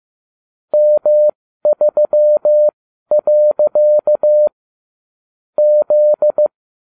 Morse code, heard as a keyed tone, about 15 wpm, spaced as for 12 wpm, 600 Hz. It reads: M3. Z